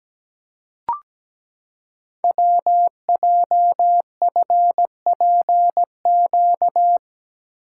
Morse code, heard as a keyed tone, about 17 wpm, 700 Hz. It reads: WJFPQ
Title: Code Groups - 5 Letters - 17wpm